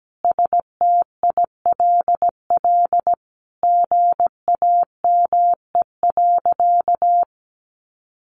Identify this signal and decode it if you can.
Morse code, heard as a keyed tone, about 17 wpm, 700 Hz. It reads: STILL GAME.
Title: Sentences from Top 500 Words 17wpm